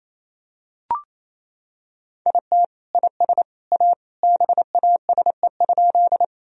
Morse code, heard as a keyed tone, about 28 wpm, 700 Hz. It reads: IT IS A BASE?
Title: Sentences from Top 1000 Words - Review Part 1 - 28wpm